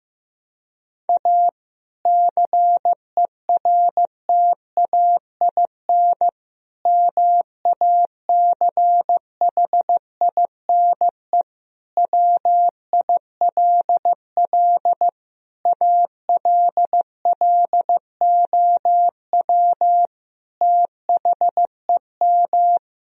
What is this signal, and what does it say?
Morse code, heard as a keyed tone, about 15 wpm, 700 Hz. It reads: A CERTAIN MACHINE WILL ALLOW THEM
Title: Sentences from Top 1000 Words - Review Part 2 - 15wpm